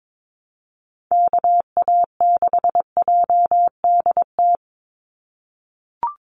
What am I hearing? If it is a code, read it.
Morse code, heard as a keyed tone, about 22 wpm, 700 Hz. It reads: KA6JDT
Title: USA Call Signs 22wpm